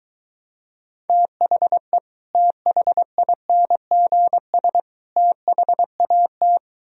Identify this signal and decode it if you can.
Morse code, heard as a keyed tone, about 23 wpm, 700 Hz. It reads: THE THINGS THAT